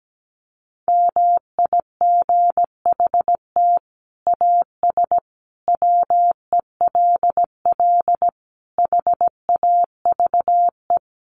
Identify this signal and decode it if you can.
Morse code, heard as a keyed tone, about 17 wpm, 700 Hz. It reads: MIGHT AS WELL HAVE
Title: Sets of 5 Words Encore2 17wpm